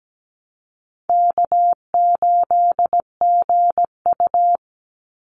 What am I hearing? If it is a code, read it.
Morse code, heard as a keyed tone, about 17 wpm, 700 Hz. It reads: K8GU
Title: Easy Call Signs Repeated 2x - 17wpm